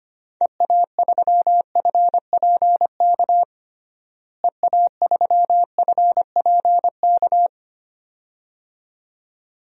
Morse code, heard as a keyed tone, about 25 wpm, 700 Hz. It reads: EA3FPK EA3FPK
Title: DXCC-100 Call Signs - 6 Characters - Repeated 2x - 25wpm